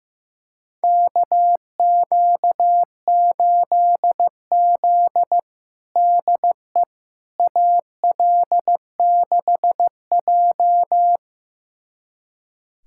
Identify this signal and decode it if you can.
Morse code, heard as a keyed tone, about 15 wpm, 700 Hz. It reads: KQ8Z DE AL6J
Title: USA Callsign DE Callsign - 4 Characters - 15wpm